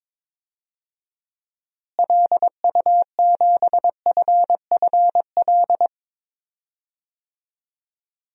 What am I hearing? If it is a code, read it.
Morse code, heard as a keyed tone, about 22 wpm, 700 Hz. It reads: LU7FFL
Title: DXCC-100 Call Signs - 6 Characters - Repeated 3x - 22wpm